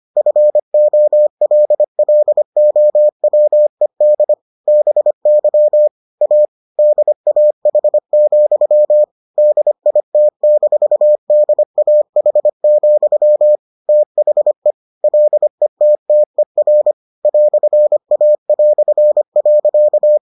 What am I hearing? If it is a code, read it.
Morse code, heard as a keyed tone, about 25 wpm, 600 Hz. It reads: FOLLOWED BY A DAH, DIT-DAH, THE LETTER "A".